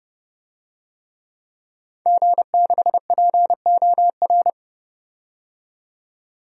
Morse code, heard as a keyed tone, about 30 wpm, 700 Hz. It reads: G6POR
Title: DXCC-100 Call Signs - 5 Characters - 30wpm